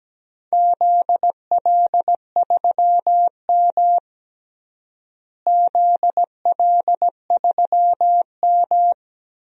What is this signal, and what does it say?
Morse code, heard as a keyed tone, about 17 wpm, 700 Hz. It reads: ZL3M ZL3M
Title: DXCC-100 Call Signs - 4 Characters - Repeated 2x - 17wpm